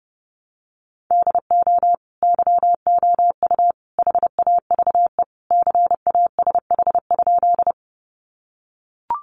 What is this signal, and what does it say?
Morse code, heard as a keyed tone, about 30 wpm, 700 Hz. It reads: DO YOU HAVE CASH?
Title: Sentences from Top 700 Words 30wpm